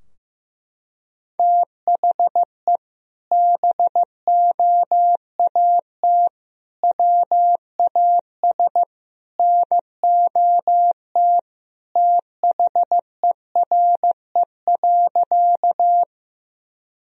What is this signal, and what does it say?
Morse code, heard as a keyed tone, about 15 wpm, 700 Hz. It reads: THE BOAT WAS NOT THERE.
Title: Sentences from Top 1000 Words - Review Part 3 - 15wpm